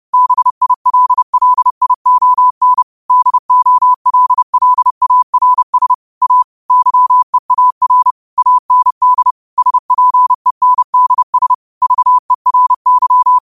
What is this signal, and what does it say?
Morse code, heard as a keyed tone, about 30 wpm, 1000 Hz. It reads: BILLION DOLLARS A YEAR AND SPENDS VERY
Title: 30 WPM morse code quotes for Fri, 01 Mar 2024